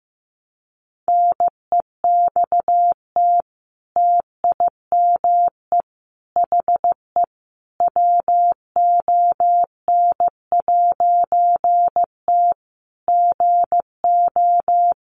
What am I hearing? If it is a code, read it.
Morse code, heard as a keyed tone, about 15 wpm, 700 Hz. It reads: NEXT TIME HE WON'T GO
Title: A Sound Of Thunder 15wpm